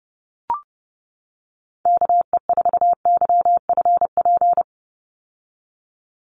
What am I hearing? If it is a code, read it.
Morse code, heard as a keyed tone, about 30 wpm, 700 Hz. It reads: KE4YFP
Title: USA Call Signs 30wpm